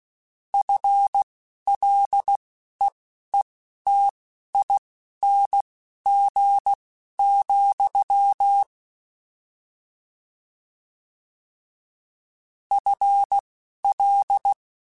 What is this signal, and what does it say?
Morse code, heard as a keyed tone, about 16 wpm, spaced as for 11 wpm, 800 Hz. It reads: FLEETING, FL